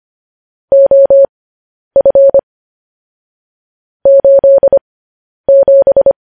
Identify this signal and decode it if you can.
Morse code, heard as a keyed tone, about 25 wpm, spaced as for 10 wpm, 550 Hz. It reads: OF 87
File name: podcast / News headlines-1x-25x10